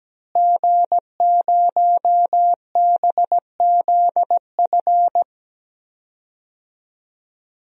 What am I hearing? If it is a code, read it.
Morse code, heard as a keyed tone, about 17 wpm, 700 Hz. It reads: G0BZF
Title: DXCC-100 Call Signs - 5 Characters - Repeated 2x - 17wpm